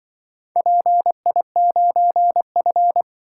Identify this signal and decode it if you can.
Morse code, heard as a keyed tone, about 24 wpm, 700 Hz. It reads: PI9F
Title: Speed-Racing - Four Letter-Number Combinations - 20wpm